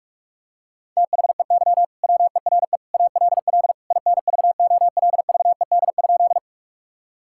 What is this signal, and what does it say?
Morse code, heard as a keyed tone, about 46 wpm, 700 Hz. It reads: THEY WERE ALL INVOLVED?
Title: Sentences from Top 1000 Words - Review Part 5 - 45wpm